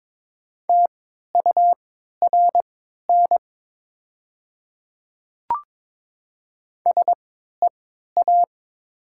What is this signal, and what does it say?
Morse code, heard as a keyed tone, about 22 wpm, spaced as for 13 wpm, 700 Hz. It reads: TURN SEA
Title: 3x Character Spacing - Top 300 Words - 22wpm